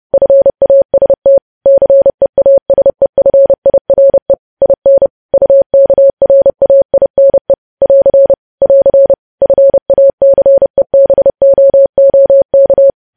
Morse code, heard as a keyed tone, about 30 wpm, 550 Hz. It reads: FAST CEASEFIRE IN UKRAINE <AR> <AR> FACEBOOK
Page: news headlines in morse code at 30 wpm